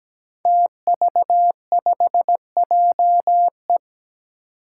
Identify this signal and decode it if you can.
Morse code, heard as a keyed tone, about 17 wpm, 700 Hz. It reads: TV5JE
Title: Five Challenging Letter-Number Combinations 17wpm